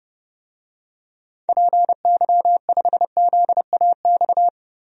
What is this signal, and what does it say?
Morse code, heard as a keyed tone, about 32 wpm, 700 Hz. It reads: PY5ZAX
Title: DXCC-100 Call Signs - 6 Characters - 30wpm